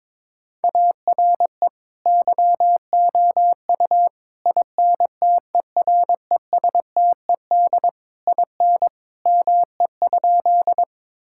Morse code, heard as a keyed tone, about 22 wpm, 700 Hz. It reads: ARE YOU INTERESTED IN ME?
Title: Sentences from Top 1000 Words - Review Part 6 - 22wpm